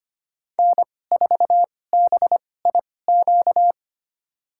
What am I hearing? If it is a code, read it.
Morse code, heard as a keyed tone, about 25 wpm, spaced as for 18 wpm, 700 Hz. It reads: N4BIQ
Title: USA Call Signs - 5 Characters - 2x Character Spacing - 25wpm